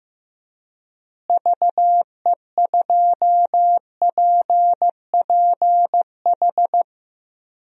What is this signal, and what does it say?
Morse code, heard as a keyed tone, about 16 wpm, 700 Hz. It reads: VE2PPH
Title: DXCC-100 Call Signs - 6 Characters - Repeated 3x - 15wpm